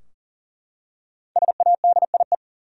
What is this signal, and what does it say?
Morse code, heard as a keyed tone, about 40 wpm, 700 Hz. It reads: SADIE
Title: USA Common Names - 40wpm